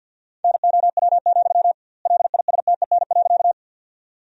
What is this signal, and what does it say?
Morse code, heard as a keyed tone, about 50 wpm, 700 Hz. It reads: NOW, LISTEN.
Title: 2x Word-Spacing - Sentences from Top 400 Words - 50wpm